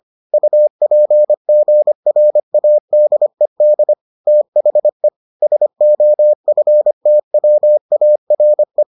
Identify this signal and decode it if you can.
Morse code, heard as a keyed tone, about 25 wpm, 600 Hz. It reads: UPGRADED THE SOFTWARE